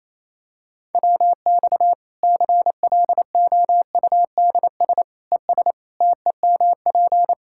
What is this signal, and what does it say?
Morse code, heard as a keyed tone, about 28 wpm, 700 Hz. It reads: WX CLOUDS ES TEMP
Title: K1USN SST QSO Practice 28wpm